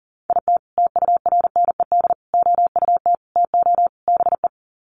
Morse code, heard as a keyed tone, about 40 wpm, 700 Hz. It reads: IT TURNED OUT TO BE
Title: Sets of 5 Words 40wpm